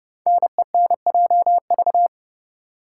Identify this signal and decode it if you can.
Morse code, heard as a keyed tone, about 31 wpm, 690 Hz. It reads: NENJV
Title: Code Groups - 5 Letters - 30wpm